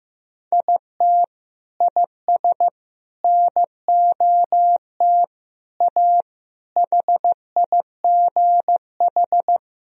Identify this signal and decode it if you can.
Morse code, heard as a keyed tone, about 15 wpm, 700 Hz. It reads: IT IS NOT A HIGH
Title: Sentences from Top 1000 Words - Review Part 4 - 15wpm